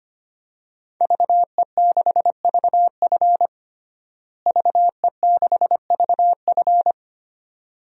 Morse code, heard as a keyed tone, about 25 wpm, 700 Hz. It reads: VE6VF VE6VF
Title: DXCC-100 Call Signs - 5 Characters - Repeated 2x - 25wpm